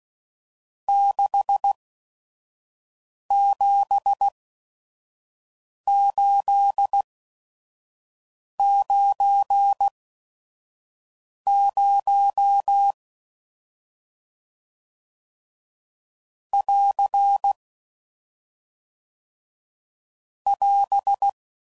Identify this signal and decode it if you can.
Morse code, heard as a keyed tone, about 16 wpm, spaced as for 5 wpm, 800 Hz. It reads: 67890 <AR> <AS>